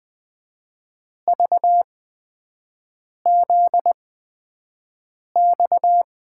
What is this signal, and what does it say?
Morse code, heard as a keyed tone, about 21 wpm, spaced as for 5 wpm, 700 Hz. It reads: VZX
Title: USA Call Signs - 8x Character Spacing - 20wpm